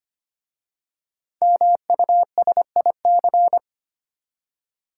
Morse code, heard as a keyed tone, about 25 wpm, 700 Hz. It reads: MUSIC
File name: practice / 25wpm / Easy Words with Hints - 25wpm